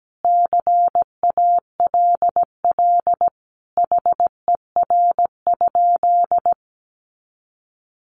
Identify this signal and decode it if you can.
Morse code, heard as a keyed tone, about 17 wpm, 700 Hz. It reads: CALL HER?
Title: Sentences from Top 100 Words 17wpm